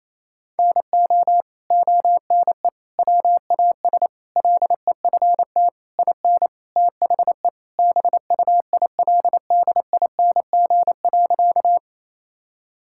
Morse code, heard as a keyed tone, about 28 wpm, 700 Hz. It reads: NO ONE WAS LEFT IN THE BUILDING.
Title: Sentences from Top 1000 Words - Review Part 5 - 28wpm